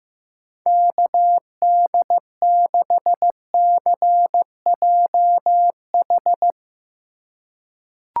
Morse code, heard as a keyed tone, about 15 wpm, 700 Hz. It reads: KD6CJH
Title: USA Call Signs - 6 Characters - 2x Character Spacing - 15wpm